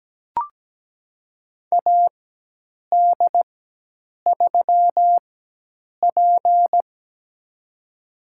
Morse code, heard as a keyed tone, about 17 wpm, spaced as for 8 wpm, 700 Hz. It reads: AD3P
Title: USA Call Signs - 4 Characters - 4x Character Spacing - 17wpm